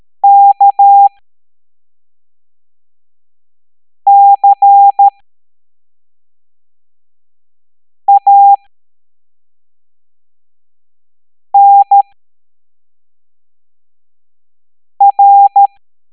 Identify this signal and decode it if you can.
Morse code, heard as a keyed tone, about 13 wpm, spaced as for 3 wpm, 800 Hz. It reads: KCANR